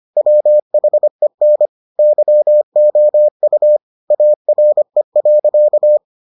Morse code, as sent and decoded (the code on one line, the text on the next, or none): .-- .... -. / -.-- --- ..- / .- .-. .-.-.-
WHEN YOU ARE.